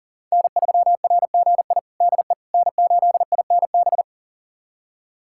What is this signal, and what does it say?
Morse code, heard as a keyed tone, about 40 wpm, 700 Hz. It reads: N3RGI DE N8INB
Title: USA Callsign DE Callsign - 5 Characters - 40wpm